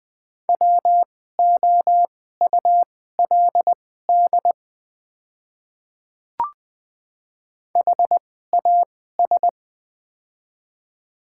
Morse code, heard as a keyed tone, about 20 wpm, spaced as for 14 wpm, 700 Hz. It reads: WOULD HAS
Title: 2x Character Spacing - Top 100 Words - 20wpm